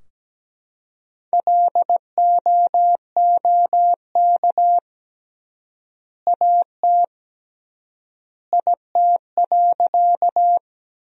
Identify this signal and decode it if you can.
Morse code, heard as a keyed tone, about 17 wpm, 700 Hz. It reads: LOOK AT IT.
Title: 3x Word-Spacing - Sentences from Top 100 Words - 17wpm